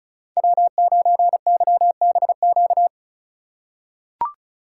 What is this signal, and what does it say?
Morse code, heard as a keyed tone, about 34 wpm, 690 Hz. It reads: W9YBQ
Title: USA Call Signs - 5 Characters - 2x Character Spacing - 35wpm